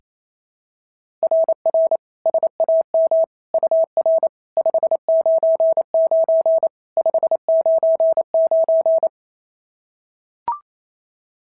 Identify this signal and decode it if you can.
Morse code, heard as a keyed tone, about 27 wpm, 690 Hz. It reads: RR SAM UR 599 599
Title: SOTA QSO Practice 28wpm